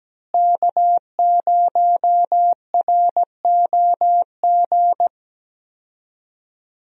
Morse code, heard as a keyed tone, about 17 wpm, 700 Hz. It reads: K0ROG
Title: USA Call Signs - 5 Characters - Repeated 3x - 17wpm